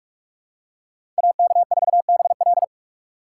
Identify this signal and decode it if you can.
Morse code, heard as a keyed tone, about 42 wpm, 700 Hz. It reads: AK4BL
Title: USA Call Signs - 5 Characters - 6x Character Spacing - 45wpm